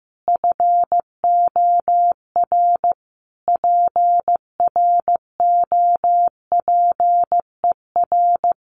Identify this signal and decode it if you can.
Morse code, heard as a keyed tone, about 14 wpm, 700 Hz. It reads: FOR PROPER
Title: Sentences from Top 1500 Words 15wpm